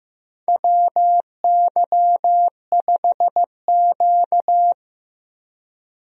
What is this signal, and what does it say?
Morse code, heard as a keyed tone, about 15 wpm, 700 Hz. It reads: WY5Q
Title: USA Call Signs - 4 Characters - Repeated 2x - 15wpm